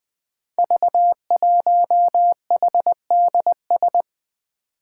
Morse code, heard as a keyed tone, about 21 wpm, 700 Hz. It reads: V1HDS
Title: Five Challenging Letter-Number Combinations 20wpm